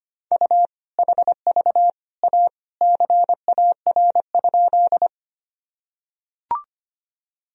Morse code, heard as a keyed tone, about 25 wpm, 700 Hz. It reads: U HV A CAR?
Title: Sentences from Top 300 Words and Abbreviations 25wpm